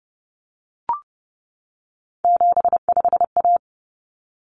Morse code, heard as a keyed tone, about 30 wpm, 700 Hz. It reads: 75A
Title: Common Cut Numbers - 3-Digit 30wpm